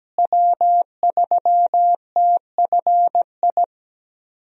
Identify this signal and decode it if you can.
Morse code, heard as a keyed tone, about 17 wpm, 700 Hz. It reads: W3TFI
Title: USA Call Signs - 5 Characters - Repeated 3x - 17wpm